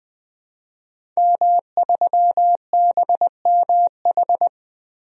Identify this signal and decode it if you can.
Morse code, heard as a keyed tone, about 20 wpm, 700 Hz. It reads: M3BMH
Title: DXCC-100 Call Signs - 5 Characters - 20wpm